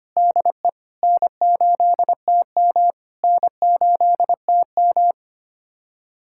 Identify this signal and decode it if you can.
Morse code, heard as a keyed tone, about 25 wpm, 690 Hz. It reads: DE N8TM N8TM
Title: SOTA QSO Practice 25wpm